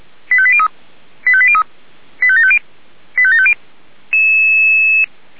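Five-tone selective call (ZVEI), heard as quick runs of five tones, tones of about 70 ms. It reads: 85803 85803 85750 85750
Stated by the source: Five-tone ZVEI selective call 85719, received by SDR.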